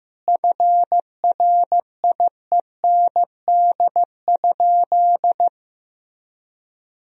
Morse code, heard as a keyed tone, about 15 wpm, 700 Hz. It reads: FRIEND?
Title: Sentences from Top 1000 Words - Review Part 3 - 15wpm